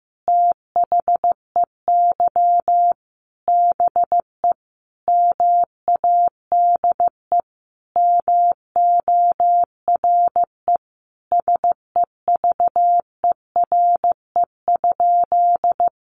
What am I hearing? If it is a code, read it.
Morse code, heard as a keyed tone, about 15 wpm, 700 Hz. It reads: THEY BE MADE MORE SEVERE?
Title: Sentences from Top 1400 Words 15wpm